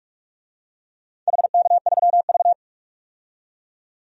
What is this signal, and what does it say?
Morse code, heard as a keyed tone, about 41 wpm, 700 Hz. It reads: HK3V
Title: DXCC-100 Call Signs - 4 Characters - 45wpm